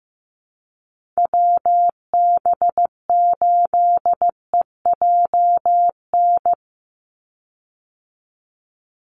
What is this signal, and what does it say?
Morse code, heard as a keyed tone, about 15 wpm, 700 Hz. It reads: WB8EJN
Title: Call Signs Repeated 3x - 15wpm